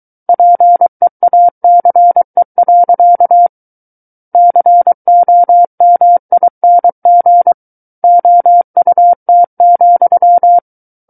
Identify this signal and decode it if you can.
Morse code, heard as a keyed tone, about 23 wpm, 700 Hz. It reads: PEACE. COMING OUT,
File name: Book One / 14-Chapter 14 - 23 WPM